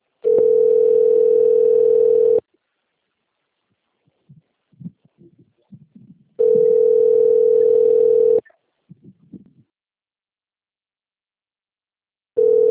Ringback tone (North America), 440 + 480 Hz.